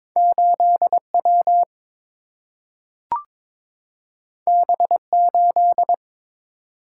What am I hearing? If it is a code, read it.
Morse code, heard as a keyed tone, about 22 wpm, 700 Hz. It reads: 8W B8
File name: practice / 22wpm / Two Letter-Number Combinations - 22wpm